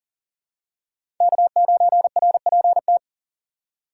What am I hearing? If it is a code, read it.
Morse code, heard as a keyed tone, about 37 wpm, 700 Hz. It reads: K9RPT
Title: USA Call Signs - 5 Characters - 4x Character Spacing - 40wpm